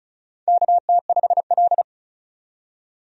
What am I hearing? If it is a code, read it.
Morse code, heard as a keyed tone, about 36 wpm, 700 Hz. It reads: KT5L
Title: USA Call Signs - 4 Characters - 35wpm